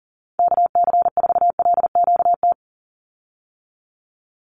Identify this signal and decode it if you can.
Morse code, heard as a keyed tone, about 40 wpm, 700 Hz. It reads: KC4LQT
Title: USA Call Signs 40wpm